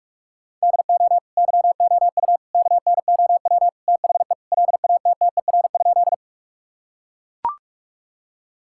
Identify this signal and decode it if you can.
Morse code, heard as a keyed tone, about 41 wpm, 700 Hz. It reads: DO YOU KNOW THE LATTER?
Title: Sentences from Top 1600 Words 45wpm